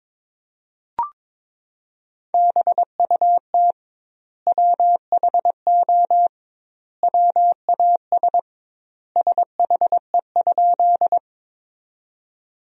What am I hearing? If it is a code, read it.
Morse code, heard as a keyed tone, about 22 wpm, 700 Hz. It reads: BUT WHO WAS SHE?
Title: 2x Word-Spacing - Sentences from Top 100 Words - 22wpm